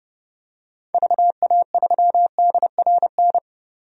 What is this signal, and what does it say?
Morse code, heard as a keyed tone, about 30 wpm, 700 Hz. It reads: VA3DRN
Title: DXCC-100 Call Signs - 6 Characters - 30wpm